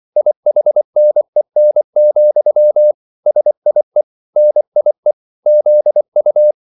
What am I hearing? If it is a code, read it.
Morse code, heard as a keyed tone, about 24 wpm, 600 Hz. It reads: IHNEN, SIE NIE ZU